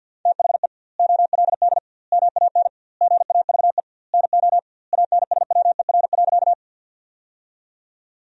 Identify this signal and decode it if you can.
Morse code, heard as a keyed tone, about 50 wpm, 700 Hz. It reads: THE OLD MAN GAVE NO ANSWER.
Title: 2x Word-Spacing - Sentences from Top 500 Words - 50wpm